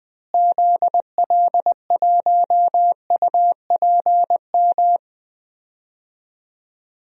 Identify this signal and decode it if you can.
Morse code, heard as a keyed tone, about 20 wpm, 700 Hz. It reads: ZL1UPM
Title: DXCC-100 Call Signs - 6 Characters - Repeated 3x - 20wpm